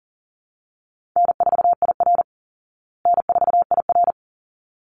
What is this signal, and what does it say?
Morse code, heard as a keyed tone, about 40 wpm, 700 Hz. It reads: N4IR N4IR
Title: Call Signs Repeated 3x - 40wpm